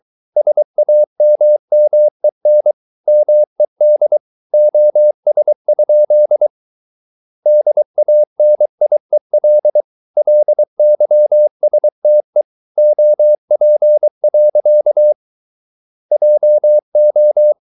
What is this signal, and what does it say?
Morse code, heard as a keyed tone, about 23 wpm, 600 Hz. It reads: SAMMEN MED OS? DANIEL LYSTE OP. JO
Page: Ferie 23 wpm